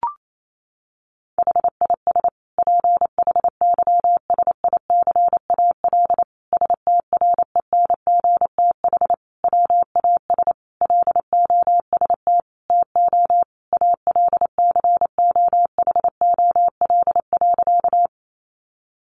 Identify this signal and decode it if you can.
Morse code, heard as a keyed tone, about 28 wpm, 700 Hz. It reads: HIS PHYSICAL STRENGTH WAS LOST TO ALCOHOL.
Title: Sentences from Top 1400 Words 28wpm